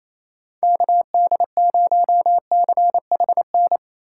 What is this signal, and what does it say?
Morse code, heard as a keyed tone, about 34 wpm, 700 Hz. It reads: KD0CHN
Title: Speed Racing - USA 2x3 Call Signs 28wpm